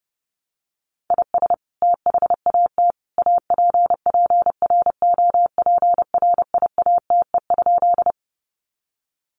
Morse code, heard as a keyed tone, about 30 wpm, 700 Hz. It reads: IS THAT APPROPRIATE?
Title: Sentences from Top 900 Words 30wpm